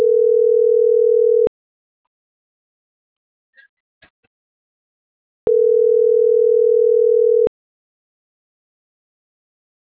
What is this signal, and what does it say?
Ringback tone (North America), 440 + 480 Hz.